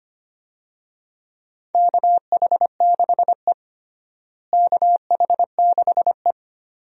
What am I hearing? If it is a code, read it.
Morse code, heard as a keyed tone, about 25 wpm, 700 Hz. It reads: KH6E KH6E
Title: DXCC-100 Call Signs - 4 Characters - Repeated 2x - 25wpm